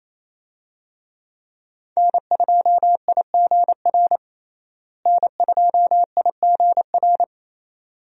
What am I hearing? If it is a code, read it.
Morse code, heard as a keyed tone, about 28 wpm, 700 Hz. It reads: N2IGR N2IGR
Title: USA Call Signs - 5 Characters - Repeated 2x - 28wpm